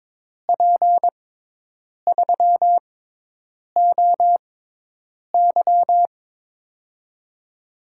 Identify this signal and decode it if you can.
Morse code, heard as a keyed tone, about 22 wpm, spaced as for 8 wpm, 700 Hz. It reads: P3OY
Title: USA Call Signs - 6x Character Spacing - 22wpm